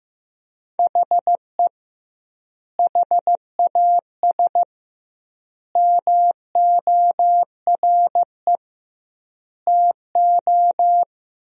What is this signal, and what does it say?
Morse code, heard as a keyed tone, about 15 wpm, 700 Hz. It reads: HE HAS MORE TO
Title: 2x Word-Spacing - Sentences from Top 200 Words - 15wpm